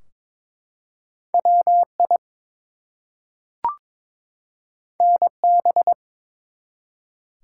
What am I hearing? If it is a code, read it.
Morse code, heard as a keyed tone, about 23 wpm, 700 Hz. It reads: WI NB
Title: US States and Canadian Provinces 22wpm